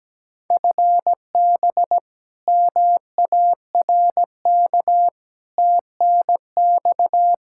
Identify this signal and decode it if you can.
Morse code, heard as a keyed tone, about 17 wpm, 700 Hz. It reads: FB MARK TNX
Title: K1USN SST QSO Practice 17wpm